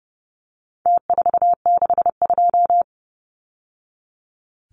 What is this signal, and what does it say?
Morse code, heard as a keyed tone, about 30 wpm, 700 Hz. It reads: T462
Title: Numbers - Contest Style - 4-Digit 30wpm